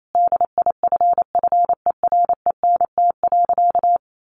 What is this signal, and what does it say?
Morse code, heard as a keyed tone, about 28 wpm, 700 Hz. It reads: DIFFERENT.
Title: Sentences from Top 600 Words 28wpm